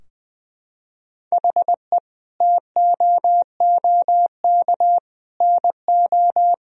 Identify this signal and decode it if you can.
Morse code, heard as a keyed tone, about 21 wpm, 700 Hz. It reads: HE TOOK NO
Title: Sentences from Top 1000 Words - Review Part 2 - 20wpm